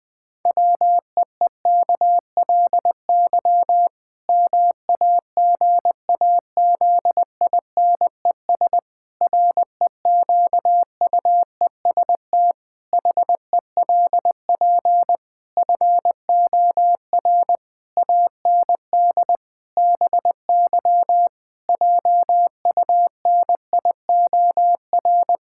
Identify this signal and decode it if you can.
Morse code, heard as a keyed tone, about 20 wpm, 700 Hz. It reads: WEEKLY MAGAZINES REQUEST HELP FOR AND BY JUNIOR
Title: Pangram Challenge v2 - 20wpm